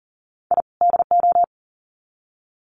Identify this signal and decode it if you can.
Morse code, heard as a keyed tone, about 38 wpm, 700 Hz. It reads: I DO
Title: Sets of 2 Words 40wpm